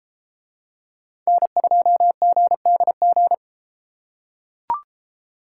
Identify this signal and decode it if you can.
Morse code, heard as a keyed tone, about 31 wpm, 700 Hz. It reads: N2GDG